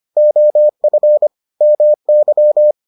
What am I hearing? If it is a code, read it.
Morse code, heard as a keyed tone, about 25 wpm, 600 Hz. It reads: OF MY